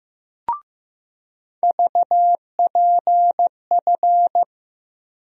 Morse code, heard as a keyed tone, about 15 wpm, 700 Hz. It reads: VPF
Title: USA Call Sign Postfixes - 3 Letters 15wpm